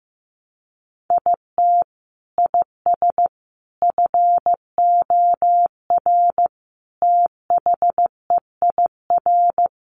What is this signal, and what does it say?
Morse code, heard as a keyed tone, about 15 wpm, 700 Hz. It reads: IT IS FOR THEIR
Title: Sentences from Top 700 Words 15wpm